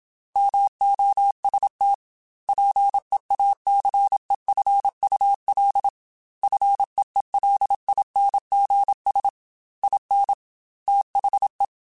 Morse code, heard as a keyed tone, about 27 wpm, 800 Hz. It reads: MOST PEACEFUL FEELINGS IN THE
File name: Quotes / Quotes-mp3s / ChildAsleep @25